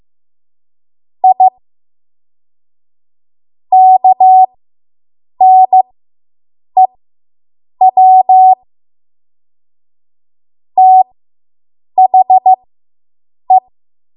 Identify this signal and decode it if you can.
Morse code, heard as a keyed tone, about 15 wpm, spaced as for 7 wpm, 750 Hz. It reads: I KNEW THE